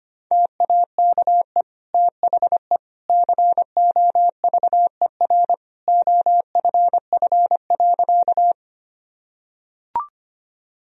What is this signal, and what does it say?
Morse code, heard as a keyed tone, about 25 wpm, 700 Hz. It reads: TAKE THE COVER OFF.
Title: Sentences from Top 1000 Words - Review Part 1 - 25wpm